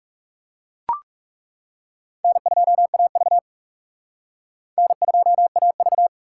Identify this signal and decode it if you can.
Morse code, heard as a keyed tone, about 36 wpm, 700 Hz. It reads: N2AV N2AV
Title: Speed Racing - USA 1x2 Call Signs 30wpm